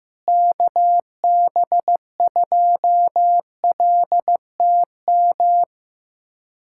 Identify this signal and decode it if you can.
Morse code, heard as a keyed tone, about 15 wpm, 700 Hz. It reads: KB2LTM
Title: USA Callsign DE Callsign 15wpm